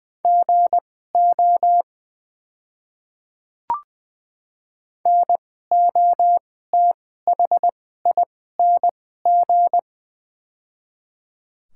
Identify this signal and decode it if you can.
Morse code, heard as a keyed tone, about 20 wpm, spaced as for 14 wpm, 700 Hz. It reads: GO NOTHING
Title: 2x Character Spacing - Top 500 Words - 20wpm